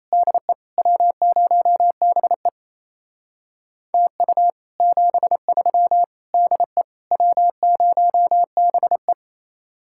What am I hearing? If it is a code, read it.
Morse code, heard as a keyed tone, about 33 wpm, 700 Hz. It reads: DE W0BE TU 73 DE W0BE
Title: Speed-Racing Common QSO Phrases Encore - 28wpm